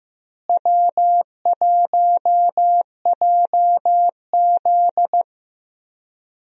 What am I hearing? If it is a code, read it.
Morse code, heard as a keyed tone, about 15 wpm, 700 Hz. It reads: W1JZ
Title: USA Call Signs - 4 Characters - Repeated 3x - 15wpm